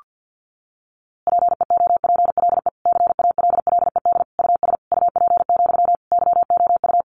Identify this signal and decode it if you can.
Morse code, heard as a keyed tone, about 50 wpm, 700 Hz. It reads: PEOPLE CALLED US UP, YOU
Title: A Sound Of Thunder 50wpm